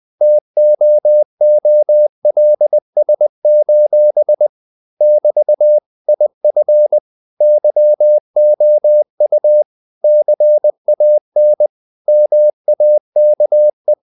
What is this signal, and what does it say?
Morse code, heard as a keyed tone, about 20 wpm, 600 Hz. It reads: TOOLS: <BT> IF YOU CAN MAKE
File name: CW / Practice / If20